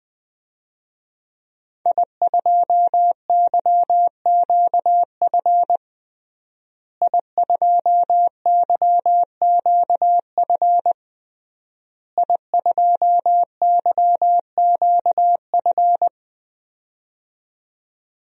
Morse code, heard as a keyed tone, about 20 wpm, 700 Hz. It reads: I2YQF I2YQF I2YQF
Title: DXCC-100 Call Signs - 5 Characters - Repeated 3x - 20wpm